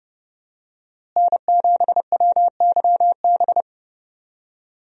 Morse code, heard as a keyed tone, about 32 wpm, 700 Hz. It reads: N7WYB
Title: USA Call Signs - 5 Characters - Repeated 3x - 30wpm